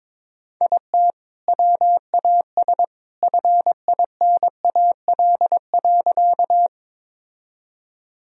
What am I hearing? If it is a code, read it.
Morse code, heard as a keyed tone, about 22 wpm, 700 Hz. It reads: IT WAS FINAL.
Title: Sentences from Top 1000 Words - Review Part 2 - 22wpm